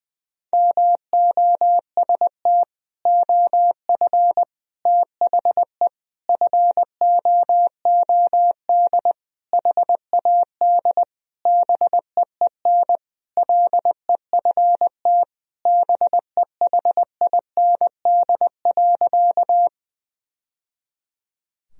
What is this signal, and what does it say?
Morse code, heard as a keyed tone, about 20 wpm, 700 Hz. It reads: MOST OF THE FOOD HAD BEEN LEFT BEHIND.
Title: Sentences from Top 1000 Words - Review Part 1 - 20wpm